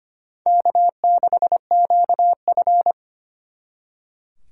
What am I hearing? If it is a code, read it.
Morse code, heard as a keyed tone, about 31 wpm, 700 Hz. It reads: K6QF
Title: Speed Racing - USA 1x2 Call Signs 25wpm